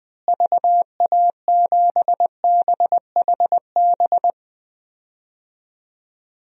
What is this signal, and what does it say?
Morse code, heard as a keyed tone, about 20 wpm, 700 Hz. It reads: VA7BHB
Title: DXCC-100 Call Signs - 6 Characters - Repeated 3x - 20wpm